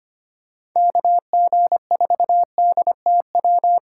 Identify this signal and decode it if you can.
Morse code, heard as a keyed tone, about 25 wpm, 700 Hz. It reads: KG4DTW
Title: USA Call Signs - 6 Characters - 6x Character Spacing - 25wpm